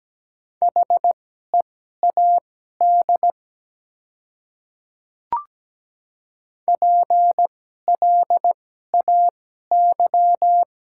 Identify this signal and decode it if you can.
Morse code, heard as a keyed tone, about 17 wpm, spaced as for 12 wpm, 700 Hz. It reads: HEAD PLAY
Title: 2x Character Spacing - Top 300 Words - 17wpm